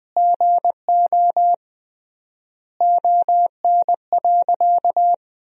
Morse code, heard as a keyed tone, about 21 wpm, 700 Hz. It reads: GO ON.
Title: 3x Word-Spacing - Sentences from Top 200 Words - 20wpm